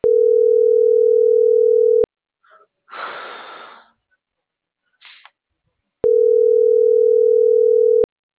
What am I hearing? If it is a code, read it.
Ringback tone (North America), 440 + 480 Hz.